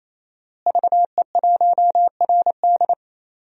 Morse code, heard as a keyed tone, about 27 wpm, 700 Hz. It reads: VE1RD
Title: DXCC-100 Call Signs - 5 Characters - Repeated 3x - 28wpm